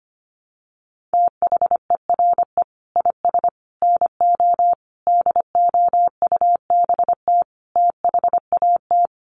Morse code, heard as a keyed tone, about 25 wpm, 700 Hz. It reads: THERE IS NO DOUBT THAT
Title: Sets of 5 Words 25wpm